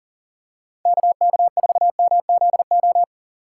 Morse code, heard as a keyed tone, about 40 wpm, 700 Hz. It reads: KK4MZO
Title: USA Call Signs - 6 Characters - 2x Character Spacing - 40wpm